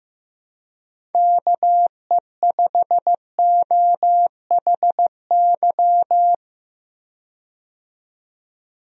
Morse code, heard as a keyed tone, about 15 wpm, 700 Hz. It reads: KE5OHY
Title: USA Call Signs - 6 Characters - 15wpm